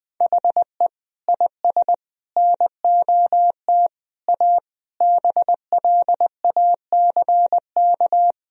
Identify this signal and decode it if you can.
Morse code, heard as a keyed tone, about 20 wpm, 700 Hz. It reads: HE IS NOT A BLACK
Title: Sentences from Top 1000 Words - Review Part 2 - 20wpm